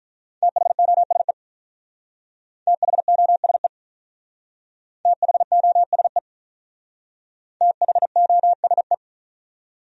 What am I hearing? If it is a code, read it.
Morse code, heard as a keyed tone, about 46 wpm, 700 Hz. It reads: THOSE THOSE THOSE THOSE